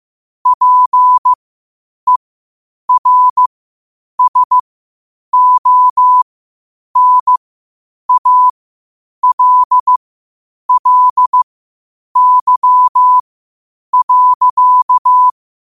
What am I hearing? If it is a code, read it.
Morse code, heard as a keyed tone, about 15 wpm, spaced as for 8 wpm, 1000 Hz. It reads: PERSONALLY.
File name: qotdfiles / 8wpm